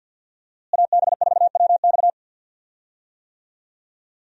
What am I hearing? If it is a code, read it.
Morse code, heard as a keyed tone, about 50 wpm, 700 Hz. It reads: AB4WX
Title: USA Call Signs - 5 Characters - Repeated 3x - 50wpm